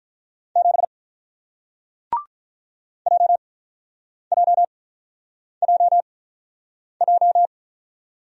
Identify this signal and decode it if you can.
Morse code, heard as a keyed tone, about 48 wpm, spaced as for 9 wpm, 700 Hz. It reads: 7 JJJJ